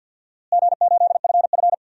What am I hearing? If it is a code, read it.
Morse code, heard as a keyed tone, about 51 wpm, 700 Hz. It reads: G8FF
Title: DXCC-100 Call Signs - 4 Characters - Repeated 2x - 50wpm